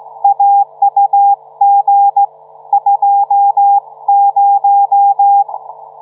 Morse code, heard as a keyed tone, about 16 wpm, 800 Hz. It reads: AUG 20